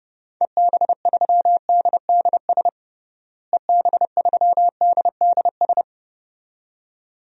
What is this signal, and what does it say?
Morse code, heard as a keyed tone, about 32 wpm, 700 Hz. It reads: EB3DDS EB3DDS
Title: DXCC-100 Call Signs - 6 Characters - Repeated 2x - 30wpm